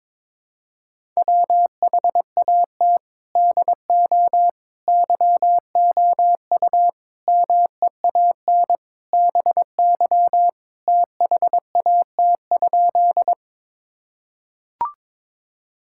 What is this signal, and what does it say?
Morse code, heard as a keyed tone, about 21 wpm, 700 Hz. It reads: WHAT DO YOU MEAN BY THAT?
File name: practice / 22wpm / Sentences with 4-Word Phrases - 22wpm